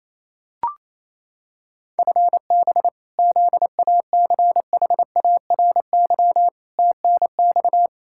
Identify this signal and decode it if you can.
Morse code, heard as a keyed tone, about 27 wpm, 700 Hz. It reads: FB ZACHARY TNX
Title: Practice QSO - Exchange 3 - More Info - 28wpm